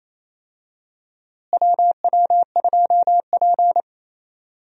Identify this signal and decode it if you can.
Morse code, heard as a keyed tone, about 28 wpm, 700 Hz. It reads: WW2P
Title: USA Call Signs - 4 Characters - Repeated 2x - 28wpm